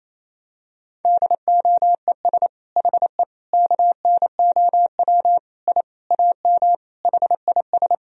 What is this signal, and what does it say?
Morse code, heard as a keyed tone, about 28 wpm, 700 Hz. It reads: DOES HE KNOW I AM HIS
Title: Sentences from Top 1000 Words - Review Part 2 - 28wpm